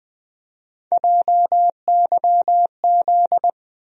Morse code, heard as a keyed tone, about 20 wpm, 700 Hz. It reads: JYZ
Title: USA Call Sign Postfixes - 3 Letters 20wpm